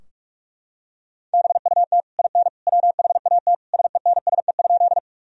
Morse code, heard as a keyed tone, about 46 wpm, 700 Hz. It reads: BUT IN WHAT SENSE?
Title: Sentences from Top 1000 Words - Review Part 4 - 45wpm